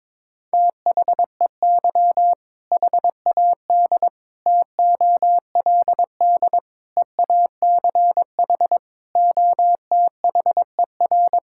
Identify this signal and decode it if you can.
Morse code, heard as a keyed tone, about 22 wpm, 700 Hz. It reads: THEY HAD TOLD EACH OTHER
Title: Sentences from Top 1600 Words 22wpm